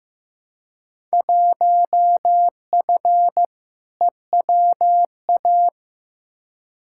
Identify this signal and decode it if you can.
Morse code, heard as a keyed tone, about 15 wpm, 700 Hz. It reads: 1F EWA
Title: Field Day Exchanges Encore - 15wpm